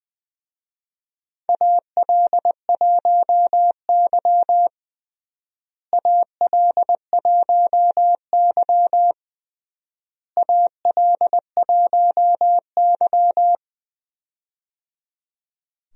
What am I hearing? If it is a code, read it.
Morse code, heard as a keyed tone, about 20 wpm, 700 Hz. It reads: AL1Y AL1Y AL1Y
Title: USA Call Signs - 4 Characters - Repeated 3x - 20wpm